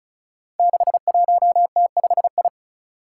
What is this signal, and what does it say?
Morse code, heard as a keyed tone, about 34 wpm, 700 Hz. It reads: 61T5I
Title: Five Challenging Letter-Number Combinations 35wpm